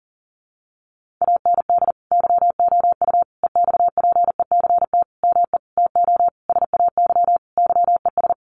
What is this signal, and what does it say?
Morse code, heard as a keyed tone, about 40 wpm, 700 Hz. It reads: AND YOU EXPECT ME TO SAY YES
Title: Sentences from Top 1100 Words 40wpm